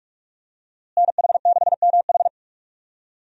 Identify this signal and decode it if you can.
Morse code, heard as a keyed tone, about 44 wpm, 700 Hz. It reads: NH6MH
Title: USA Call Signs - 5 Characters - Repeated 2x - 45wpm